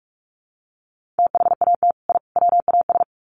Morse code, heard as a keyed tone, about 44 wpm, 700 Hz. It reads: THAT I WAS
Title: Sets of 3 Words 45wpm